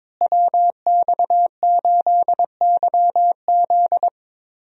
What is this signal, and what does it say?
Morse code, heard as a keyed tone, about 22 wpm, 700 Hz. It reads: WX8YZ
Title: USA Call Signs - 5 Characters - Repeated 2x - 22wpm